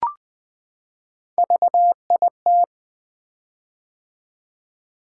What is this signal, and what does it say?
Morse code, heard as a keyed tone, about 21 wpm, 700 Hz. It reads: VIT
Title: Three Letter-Number Combinations - 20wpm